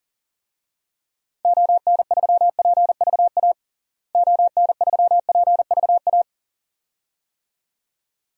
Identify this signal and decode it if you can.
Morse code, heard as a keyed tone, about 40 wpm, 700 Hz. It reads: ON3PVA ON3PVA
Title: DXCC-100 Call Signs - 6 Characters - Repeated 2x - 40wpm